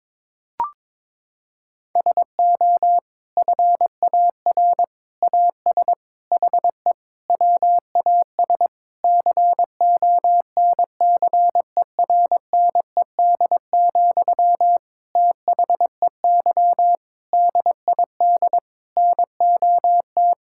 Morse code, heard as a keyed tone, about 22 wpm, 700 Hz. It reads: SO FAR AS HE WAS CONCERNED, THEY DID NOT
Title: Sentences from Top 1000 Words - Review Part 6 - 22wpm